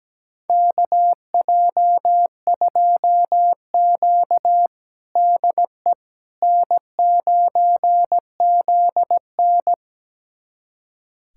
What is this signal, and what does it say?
Morse code, heard as a keyed tone, about 17 wpm, 700 Hz. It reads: KJ2Q DE N9ZN
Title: USA Callsign DE Callsign - 4 Characters - 17wpm